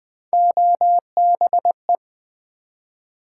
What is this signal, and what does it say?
Morse code, heard as a keyed tone, about 20 wpm, 700 Hz. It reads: OBE
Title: Build A Word - 1 Letter At A Time - 3 to 5-Letter Words - 20wpm